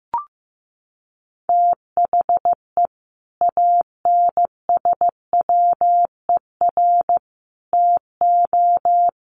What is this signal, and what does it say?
Morse code, heard as a keyed tone, about 15 wpm, 700 Hz. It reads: THE ANSWER TO
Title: Sets of 5 Words Encore2 15wpm